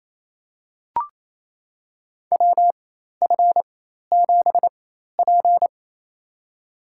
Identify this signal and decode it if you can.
Morse code, heard as a keyed tone, about 28 wpm, spaced as for 13 wpm, 700 Hz. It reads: WF7P
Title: USA Call Signs - 4x Character Spacing - 28wpm